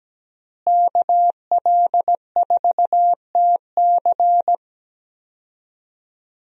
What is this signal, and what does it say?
Morse code, heard as a keyed tone, about 17 wpm, 700 Hz. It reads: KL4TC
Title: USA Call Signs - 5 Characters - Repeated 3x - 17wpm